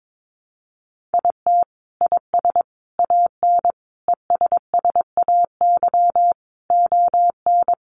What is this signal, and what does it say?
Morse code, heard as a keyed tone, about 22 wpm, 700 Hz. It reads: IT IS AN ESSAY ON
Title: Sentences from Top 1500 Words 22wpm